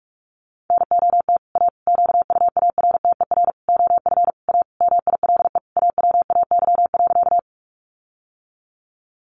Morse code, heard as a keyed tone, about 44 wpm, 700 Hz. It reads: NOT A QUARTER OF A MILE AWAY.
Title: Sentences from Top 1300 Words 45wpm